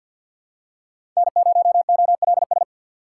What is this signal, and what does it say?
Morse code, heard as a keyed tone, about 53 wpm, 700 Hz. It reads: N0OLS
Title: USA Call Signs - 5 Characters - 4x Character Spacing - 50wpm